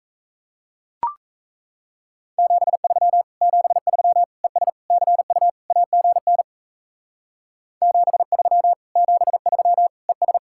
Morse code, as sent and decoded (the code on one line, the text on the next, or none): --... ...-- / --... ...-- / . ... / -.-. ..- / .- --. -. / --... ...-- / --... ...-- / . ...
73 73 ES CU AGN 73 73 ES